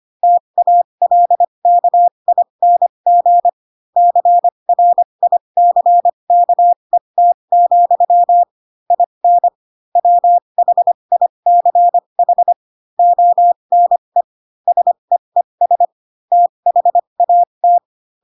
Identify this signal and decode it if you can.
Morse code, heard as a keyed tone, about 25 wpm, spaced as for 22 wpm, 700 Hz. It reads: TALKING CRICKET, IN WHICH ONE SEES THAT
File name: Pinocchio / Pinocchio ch4 18 25 ik1hge